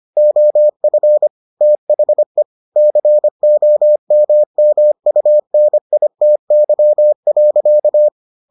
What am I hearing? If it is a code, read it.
Morse code, heard as a keyed tone, about 25 wpm, 600 Hz. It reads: OF THE COMMUNITY.